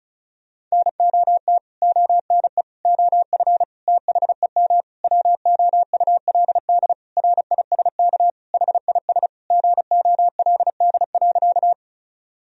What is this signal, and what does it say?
Morse code, heard as a keyed tone, about 34 wpm, 700 Hz. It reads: NOT ONE OF THEM WOULD RISK HIS GOLD.
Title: Sentences from Top 1000 Words - Review Part 4 - 35wpm